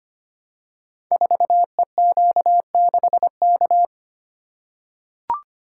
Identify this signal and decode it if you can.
Morse code, heard as a keyed tone, about 25 wpm, 700 Hz. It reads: 4EQ6K
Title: Code Groups - 5 Letter:Numbers - 25wpm